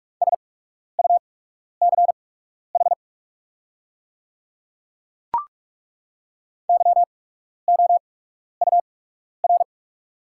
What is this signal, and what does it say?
Morse code, heard as a keyed tone, about 48 wpm, spaced as for 12 wpm, 690 Hz. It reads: SUCH YOUR
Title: 8x Character Spacing - Top 200 Words - 45wpm